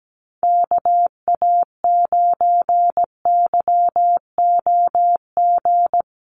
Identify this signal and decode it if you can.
Morse code, heard as a keyed tone, about 17 wpm, 700 Hz. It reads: KA9YOG
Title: USA Call Signs Repeated 2x 17wpm